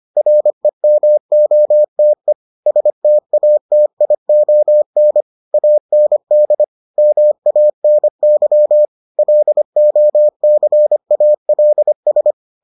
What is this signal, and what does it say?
Morse code, heard as a keyed tone, about 25 wpm, 600 Hz. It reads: REMOTE STATION AND MANY LOCALS